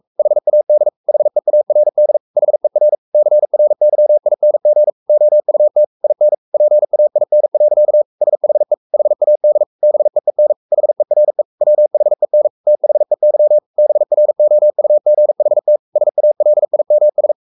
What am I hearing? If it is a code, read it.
Morse code, heard as a keyed tone, about 44 wpm, 600 Hz. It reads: HAD HEARD HER CRYING OUT IN PAIN. SHE HAD BEEN HERE WHEN THEY BROUGHT SALIMS
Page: Romanen Never af Ken Follet på engelsk.